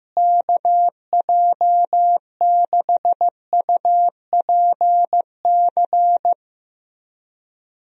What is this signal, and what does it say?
Morse code, heard as a keyed tone, about 16 wpm, 700 Hz. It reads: KJ6UPC
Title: USA Call Signs - 6 Characters - Repeated 2x - 15wpm